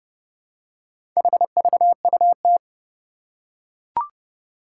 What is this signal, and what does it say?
Morse code, heard as a keyed tone, about 30 wpm, 700 Hz. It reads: HVUT
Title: Four Challenging Letter-Number Combinations 30wpm